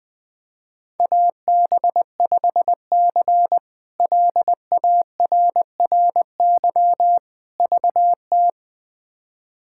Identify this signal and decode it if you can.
Morse code, heard as a keyed tone, about 21 wpm, 700 Hz. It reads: AB5C LARRY VT
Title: K1USN SST QSO Practice - 20wpm